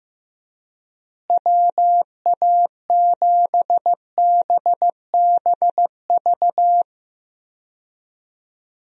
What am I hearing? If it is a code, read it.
Morse code, heard as a keyed tone, about 16 wpm, 700 Hz. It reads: WA7BBV
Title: USA Call Signs - 6 Characters - Repeated 3x - 15wpm